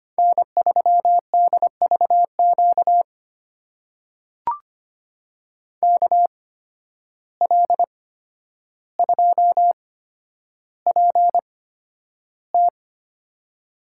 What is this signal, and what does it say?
Morse code, heard as a keyed tone, about 25 wpm, spaced as for 11 wpm, 700 Hz. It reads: N3DVQ KL2PT
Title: USA Call Signs - 5 Characters - 8x Character Spacing - 25wpm